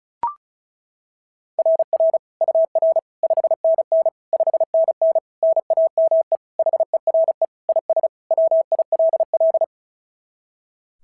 Morse code, heard as a keyed tone, about 36 wpm, 690 Hz. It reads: RR UR 5NN 5NN NAME HERE IS WILL
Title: SOTA QSO Practice 35wpm